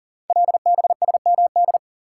Morse code, heard as a keyed tone, about 38 wpm, 700 Hz. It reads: LBSMD
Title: Code Groups - 5 Letters - 40wpm